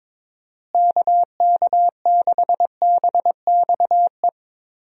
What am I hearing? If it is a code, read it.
Morse code, heard as a keyed tone, about 22 wpm, 700 Hz. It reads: KK6BXE
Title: USA Call Signs - 6 Characters - Repeated 3x - 22wpm